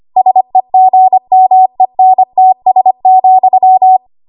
Morse code, heard as a keyed tone, about 25 wpm, 750 Hz. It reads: SEGMENTS,
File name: Morse / 25WPM